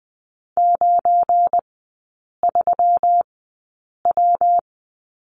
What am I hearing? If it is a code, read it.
Morse code, heard as a keyed tone, about 20 wpm, spaced as for 8 wpm, 700 Hz. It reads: 93W
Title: 191 Lesson 31e - Letter v - Three Characters 20wpm